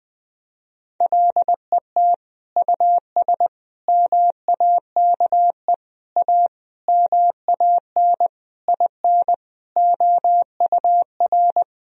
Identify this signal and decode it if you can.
Morse code, heard as a keyed tone, about 20 wpm, 700 Hz. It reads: LET US MAKE A MAN IN OUR
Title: Sentences from Top 1000 Words - Review Part 4 - 20wpm